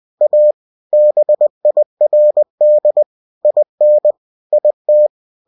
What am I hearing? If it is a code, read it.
Morse code, heard as a keyed tone, about 20 wpm, 600 Hz. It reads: A BIRD IN IT
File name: cw-books / winnie-the-pooh / winnie-20wpm-page-0069